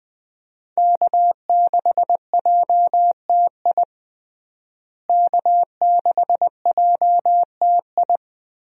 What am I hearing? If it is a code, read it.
Morse code, heard as a keyed tone, about 20 wpm, 700 Hz. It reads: K6JTI K6JTI
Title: USA Call Signs - 5 Characters - Repeated 3x - 20wpm